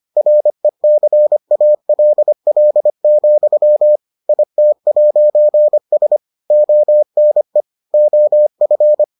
Morse code, heard as a keyed tone, about 25 wpm, 600 Hz. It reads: RECALL, IT'S ONE OF